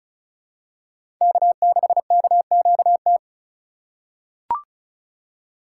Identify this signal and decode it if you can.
Morse code, heard as a keyed tone, about 34 wpm, 700 Hz. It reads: K6KQT
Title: USA Call Signs - 5 Characters - 3x Character Spacing - 35wpm